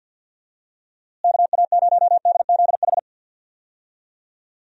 Morse code, heard as a keyed tone, about 50 wpm, 700 Hz. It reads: KA0DZH
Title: USA Call Signs - 6 Characters - Repeated 3x - 50wpm